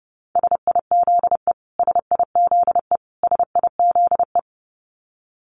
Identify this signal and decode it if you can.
Morse code, heard as a keyed tone, about 28 wpm, 700 Hz. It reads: SIZE SIZE SIZE
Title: Top 500 Words Repeated 30wpm